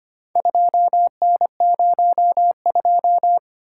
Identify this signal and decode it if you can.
Morse code, heard as a keyed tone, about 25 wpm, 700 Hz. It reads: 2N02
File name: practice / 25wpm / Four Letter-Number Combinations - 25wpm